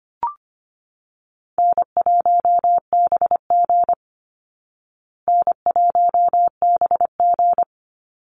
Morse code, heard as a keyed tone, about 25 wpm, 700 Hz. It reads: N1BG N1BG
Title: Call Signs Repeated 2x - 25wpm